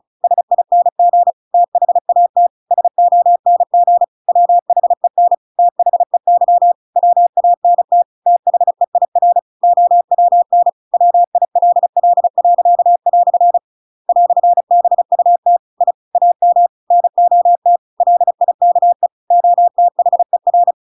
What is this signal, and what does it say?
Morse code, heard as a keyed tone, about 34 wpm, 700 Hz. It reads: SING THAT SONG WHEN THEY WANT THEIR OWN WILL." "BUT I AM NOT LIKE OTHER
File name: Pinocchio / Pinocchio ch8 35 35 ik1hge